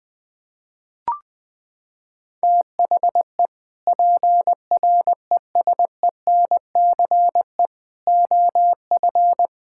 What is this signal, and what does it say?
Morse code, heard as a keyed tone, about 20 wpm, 700 Hz. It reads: THE PRESENCE OF
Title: Sentences with 3-Word Phrases - 20wpm